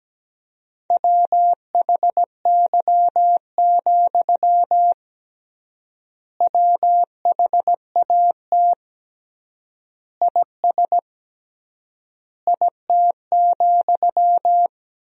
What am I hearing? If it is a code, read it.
Morse code, heard as a keyed tone, about 17 wpm, 700 Hz. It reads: WHY, WHAT IS IT,
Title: 3x Word-Spacing - Sentences from Top 200 Words - 17wpm